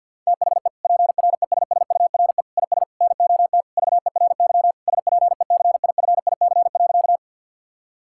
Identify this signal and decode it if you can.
Morse code, heard as a keyed tone, about 50 wpm, 700 Hz. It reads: THE PRESSURE IS NOT VERY SPECIFIC.
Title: Sentences from Top 1000 Words - Review Part 6 - 50wpm